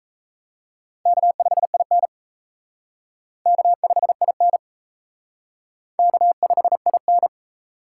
Morse code, heard as a keyed tone, about 34 wpm, 700 Hz. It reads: K5IN K5IN K5IN
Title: Speed Racing - USA 1x2 Call Signs 28wpm